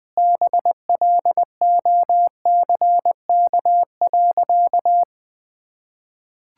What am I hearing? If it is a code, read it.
Morse code, heard as a keyed tone, about 21 wpm, 700 Hz. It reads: BLOCK.
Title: Sentences from Top 1600 Words 20wpm